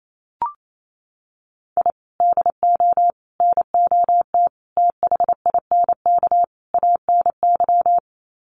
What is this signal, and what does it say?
Morse code, heard as a keyed tone, about 28 wpm, 700 Hz. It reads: I DO NOT THINK ANY
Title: Sets of 5 Words Encore 28wpm